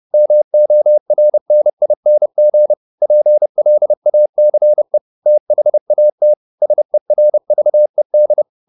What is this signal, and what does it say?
Morse code, heard as a keyed tone, about 30 wpm, 600 Hz. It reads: MORNING PLACE THAT SERVED
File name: the old man and the sea / the old man and the sea 1 30wpm